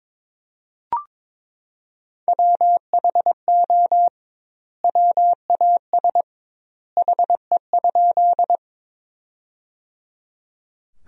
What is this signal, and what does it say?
Morse code, heard as a keyed tone, about 22 wpm, 700 Hz. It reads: WHO WAS HE?
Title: 2x Word-Spacing - Sentences from Top 100 Words - 22wpm